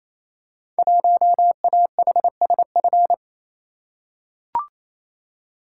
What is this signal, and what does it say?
Morse code, heard as a keyed tone, about 29 wpm, 700 Hz. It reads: 1AHSF
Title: Code Groups - 5 Letter:Numbers - 28wpm